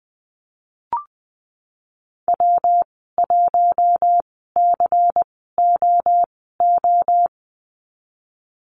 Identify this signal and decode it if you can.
Morse code, heard as a keyed tone, about 20 wpm, spaced as for 15 wpm, 700 Hz. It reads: W1COO
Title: 167 Lesson 27g - Number 7 - Call Signs 20wpm